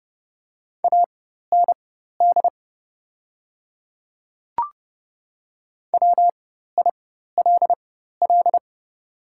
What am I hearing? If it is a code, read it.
Morse code, heard as a keyed tone, about 30 wpm, spaced as for 14 wpm, 700 Hz. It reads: AND WILL
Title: 4x Character Spacing - Top 100 Words - 30wpm